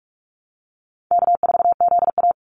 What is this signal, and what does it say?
Morse code, heard as a keyed tone, about 44 wpm, 700 Hz. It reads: K4ZA
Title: Easy Call Signs Repeated 2x - 45wpm